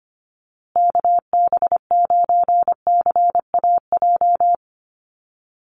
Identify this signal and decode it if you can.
Morse code, heard as a keyed tone, about 25 wpm, 700 Hz. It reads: KB9CAJ
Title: USA Call Signs Repeated 2x 25wpm